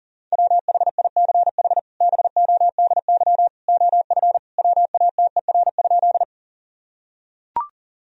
Morse code, heard as a keyed tone, about 40 wpm, 700 Hz. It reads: WHICH BODY OF WATER?
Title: Sentences from Top 1000 Words - Review Part 2 - 40wpm